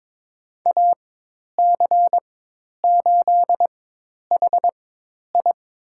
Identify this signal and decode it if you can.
Morse code, heard as a keyed tone, about 22 wpm, spaced as for 10 wpm, 700 Hz. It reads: AC8HI
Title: USA Call Signs - 5 Characters - 4x Character Spacing - 22wpm